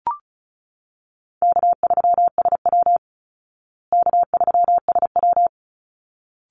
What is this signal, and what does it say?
Morse code, heard as a keyed tone, about 34 wpm, 700 Hz. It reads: K3SW K3SW
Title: Call Signs Repeated 3x - 35wpm